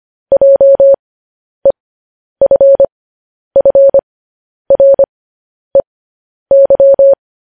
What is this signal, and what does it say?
Morse code, heard as a keyed tone, about 25 wpm, spaced as for 10 wpm, 550 Hz. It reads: JEFFREY
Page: news-headlines-in-morse-code-at-10-wpm